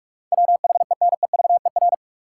Morse code, heard as a keyed tone, about 46 wpm, 700 Hz. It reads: WHENEVER
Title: Sentences from Top 1200 Words - New Word Drill Alt - 45wpm